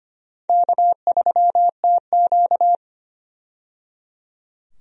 Morse code, heard as a keyed tone, about 25 wpm, 700 Hz. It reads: K3TQ
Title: USA Call Signs - 4 Characters - 25wpm